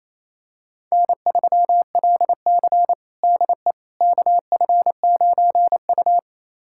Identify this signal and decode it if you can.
Morse code, heard as a keyed tone, about 28 wpm, 700 Hz. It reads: N3LC DE KF9U
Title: USA Callsign DE Callsign - 4 Characters - 28wpm